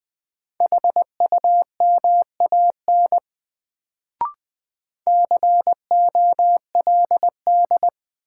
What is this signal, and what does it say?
Morse code, heard as a keyed tone, about 20 wpm, 700 Hz. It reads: HUMAN COLD
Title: Sets of 5 Words - New Word Drill - 20wpm